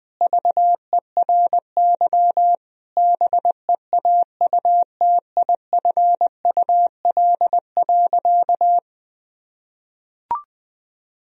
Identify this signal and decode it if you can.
Morse code, heard as a keyed tone, about 20 wpm, 700 Hz. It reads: VERY BEAUTIFUL.
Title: Sentences from Top 1000 Words - Review Part 4 - 20wpm